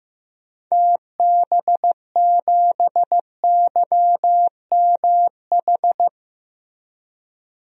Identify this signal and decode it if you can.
Morse code, heard as a keyed tone, about 15 wpm, 700 Hz. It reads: TB7YMH
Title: DXCC-100 Call Signs - 6 Characters - Repeated 3x - 15wpm